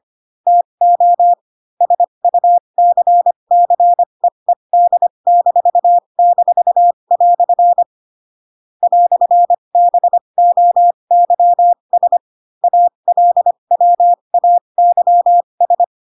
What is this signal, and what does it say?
Morse code, heard as a keyed tone, about 25 wpm, spaced as for 22 wpm, 700 Hz. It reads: TO SUCCEED--" "BOYS ALWAYS